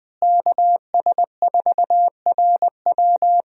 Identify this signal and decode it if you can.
Morse code, heard as a keyed tone, about 20 wpm, 700 Hz. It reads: KS4RW
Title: USA Call Signs - 5 Characters - Repeated 3x - 20wpm